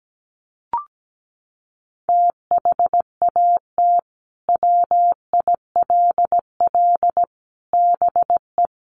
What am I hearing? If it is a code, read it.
Morse code, heard as a keyed tone, about 17 wpm, 700 Hz. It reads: THAT WILL BE
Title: Sentences from Top 1400 Words 17wpm